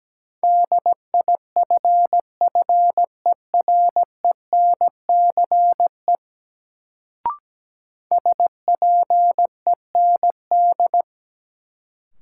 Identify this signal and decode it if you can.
Morse code, heard as a keyed tone, about 17 wpm, 700 Hz. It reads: DIFFERENCE SPEND
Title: Sentences from Top 800 Words - New Word Drill - 17wpm